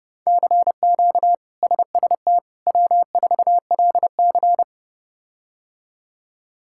Morse code, heard as a keyed tone, about 30 wpm, 700 Hz. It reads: CQ SST W4LC
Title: K1USN SST QSO Practice - 30wpm